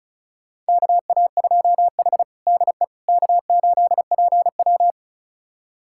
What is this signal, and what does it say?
Morse code, heard as a keyed tone, about 34 wpm, 700 Hz. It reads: KA2H DE K8PW
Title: USA Callsign DE Callsign - 4 Characters - 35wpm